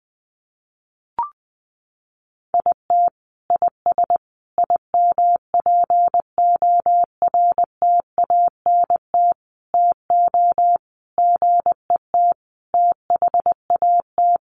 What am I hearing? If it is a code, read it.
Morse code, heard as a keyed tone, about 20 wpm, 700 Hz. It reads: IT IS IMPORTANT TO GET THAT
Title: Sentences from Top 1100 Words 20wpm